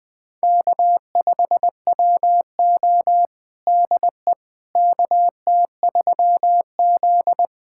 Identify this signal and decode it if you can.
Morse code, heard as a keyed tone, about 20 wpm, 700 Hz. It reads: K5WO DE KT3Z
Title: USA Callsign DE Callsign - 4 Characters - 20wpm